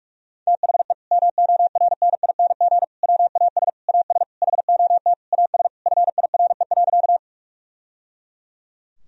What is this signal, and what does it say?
Morse code, heard as a keyed tone, about 48 wpm, 700 Hz. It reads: THE MORNING WAS AS HOT AS FIRE.
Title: Sentences from Top 1000 Words - Review Part 3 - 45wpm